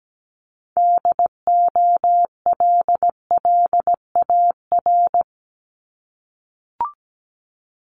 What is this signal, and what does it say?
Morse code, heard as a keyed tone, about 17 wpm, 700 Hz. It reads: DOLLAR
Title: Multi-letter Phonemes 17wpm